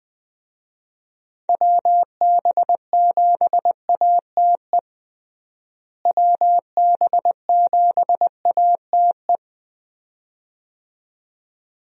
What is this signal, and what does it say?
Morse code, heard as a keyed tone, about 20 wpm, 700 Hz. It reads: WB7ATE WB7ATE
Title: USA Call Signs - 6 Characters - Repeated 2x - 20wpm